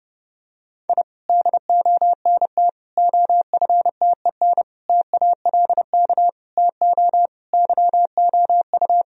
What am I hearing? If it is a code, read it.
Morse code, heard as a keyed tone, about 30 wpm, 700 Hz. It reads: I DONT OFTEN TALK TO YOU
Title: Sentences from Top 1000 Words - Review Part 1 - 30wpm